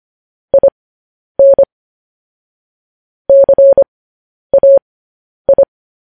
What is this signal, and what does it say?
Morse code, heard as a keyed tone, about 25 wpm, spaced as for 10 wpm, 550 Hz. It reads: IN CAI
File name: podcast / News headlines-1x-25x10